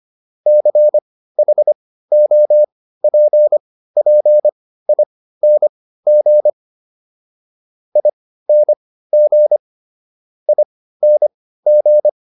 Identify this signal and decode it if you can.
Morse code, heard as a keyed tone, about 25 wpm, spaced as for 15 wpm, 600 Hz. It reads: CHOPPING ING ING